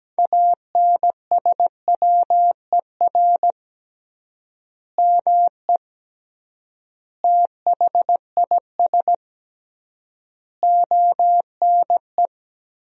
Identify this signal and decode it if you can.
Morse code, heard as a keyed tone, about 17 wpm, 700 Hz. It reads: ANSWER ME THIS ONE
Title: 3x Word-Spacing - Sentences from Top 400 Words - 17wpm